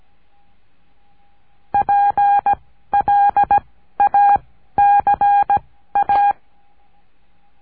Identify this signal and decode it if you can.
Morse code, heard as a keyed tone, about 17 wpm, 810 Hz. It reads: P L A C A
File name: cursos / CURSO DE TELEGRAFIA / Telegrafia - Trilha 14